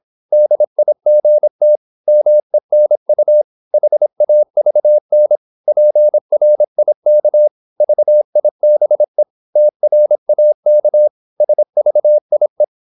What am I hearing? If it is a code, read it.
Morse code, heard as a keyed tone, about 26 wpm, 600 Hz. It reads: DIGT MENU HAVN PRIK VIBE TRAK SVIE
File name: Korte ord DK 26wpm